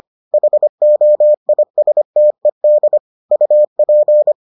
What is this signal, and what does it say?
Morse code, heard as a keyed tone, about 25 wpm, 600 Hz. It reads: HOISTED UP